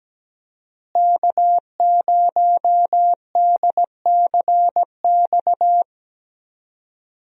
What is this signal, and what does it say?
Morse code, heard as a keyed tone, about 17 wpm, 700 Hz. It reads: K0DCX
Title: USA Call Signs - 5 Characters - Repeated 2x - 17wpm